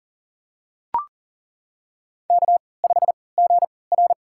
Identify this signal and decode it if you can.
Morse code, heard as a keyed tone, about 40 wpm, spaced as for 23 wpm, 700 Hz. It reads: K5GR
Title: USA Call Signs - 4 Characters - 3x Character Spacing - 40wpm